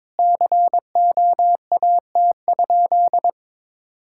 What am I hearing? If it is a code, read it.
Morse code, heard as a keyed tone, about 22 wpm, 700 Hz. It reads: COAT?
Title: Sentences from Top 1600 Words 22wpm